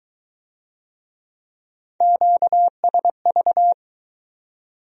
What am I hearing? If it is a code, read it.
Morse code, heard as a keyed tone, about 19 wpm, 700 Hz. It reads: QSV